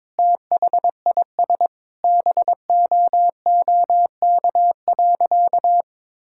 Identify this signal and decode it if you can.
Morse code, heard as a keyed tone, about 22 wpm, 700 Hz. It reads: THIS BOOK.
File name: practice / 22wpm / Sentences with 2-Word Phrases - 22wpm